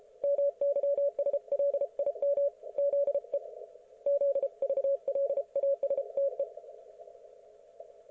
Morse code, heard as a keyed tone, about 32 wpm, 560 Hz. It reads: MYSLIM ZE ZVLASTE